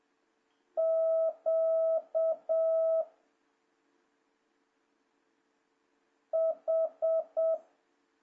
Morse code, heard as a keyed tone, about 7 wpm, 650 Hz. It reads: Q H
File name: jh8cbh / 2026cw